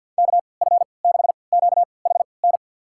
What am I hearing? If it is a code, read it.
Morse code, heard as a keyed tone, about 52 wpm, spaced as for 29 wpm, 700 Hz. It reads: KF6QHN
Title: USA Call Signs - 6 Characters - 3x Character Spacing - 50wpm